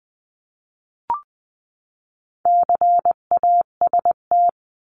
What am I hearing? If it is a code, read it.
Morse code, heard as a keyed tone, about 20 wpm, 700 Hz. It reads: CAST
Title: 259 Lesson 42d - Words 18-20wpm